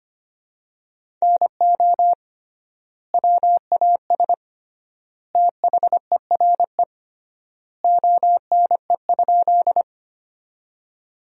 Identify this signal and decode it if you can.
Morse code, heard as a keyed tone, about 25 wpm, 700 Hz. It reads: NO WAS THERE ONE?
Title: 3x Word-Spacing - Sentences from Top 100 Words - 25wpm